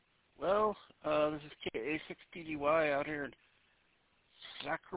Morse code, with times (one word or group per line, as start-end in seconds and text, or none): none